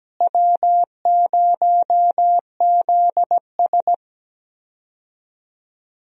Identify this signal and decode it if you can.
Morse code, heard as a keyed tone, about 17 wpm, 700 Hz. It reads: W0ZS
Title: USA Call Signs - 4 Characters - Repeated 3x - 17wpm